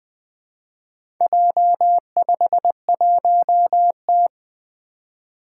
Morse code, heard as a keyed tone, about 20 wpm, 700 Hz. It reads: J51T
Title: Four Challenging Letter-Number Combinations 20wpm